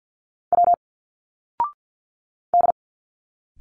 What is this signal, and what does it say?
Morse code, heard as a keyed tone, about 47 wpm, 700 Hz. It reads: W D
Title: Single Letters 50wpm